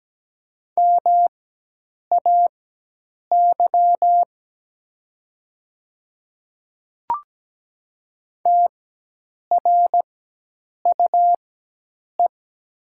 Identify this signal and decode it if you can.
Morse code, heard as a keyed tone, about 17 wpm, spaced as for 8 wpm, 700 Hz. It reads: MAY TRUE
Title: 4x Character Spacing - Top 500 Words - 17wpm